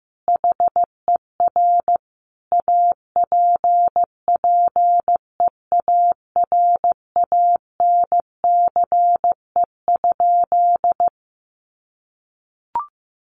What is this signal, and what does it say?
Morse code, heard as a keyed tone, about 15 wpm, 700 Hz. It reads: HER APPEARANCE?
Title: Sentences from Top 1100 Words 15wpm